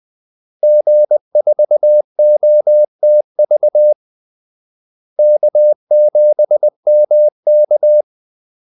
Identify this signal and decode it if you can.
Morse code, heard as a keyed tone, about 20 wpm, 600 Hz. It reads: G4OTV K7MK
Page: Kaldesignaler 20 wpm